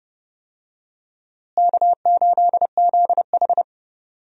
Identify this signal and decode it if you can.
Morse code, heard as a keyed tone, about 32 wpm, 700 Hz. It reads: K8ZH
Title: USA Call Signs - 4 Characters - Repeated 3x - 30wpm